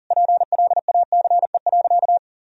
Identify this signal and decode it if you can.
Morse code, heard as a keyed tone, about 40 wpm, 700 Hz. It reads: PLACE.
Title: Sentences from Top 1000 Words - Review Part 1 - 40wpm